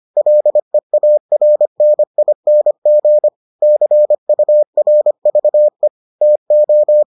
Morse code, heard as a keyed tone, about 25 wpm, 600 Hz. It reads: LEARNING CURVE TO